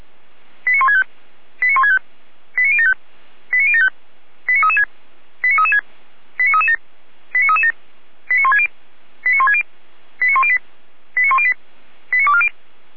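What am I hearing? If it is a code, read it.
Five-tone selective call (ZVEI), heard as quick runs of five tones, tones of about 70 ms. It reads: 89165 89165 89075 89075 89336 89336 89337 89337 89160 89160 89108 89108 89240
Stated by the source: ZVEI five-tone selective call 89073 received with an SDR, served by feuerwehr-orlamuende.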